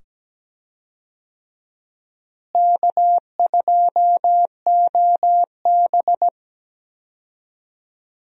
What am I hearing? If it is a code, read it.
Morse code, heard as a keyed tone, about 17 wpm, 700 Hz. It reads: K2OB
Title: USA Call Signs - 4 Characters - Repeated 2x - 17wpm